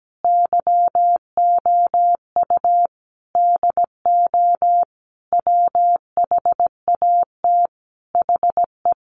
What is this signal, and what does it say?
Morse code, heard as a keyed tone, about 17 wpm, 700 Hz. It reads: YOU DO WHAT HE
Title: Sentences from Top 100 Words 17wpm